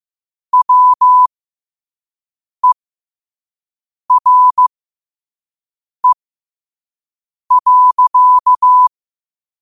Morse code, heard as a keyed tone, about 15 wpm, spaced as for 5 wpm, 1000 Hz. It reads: WERE.
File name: qotdfiles / 5wpm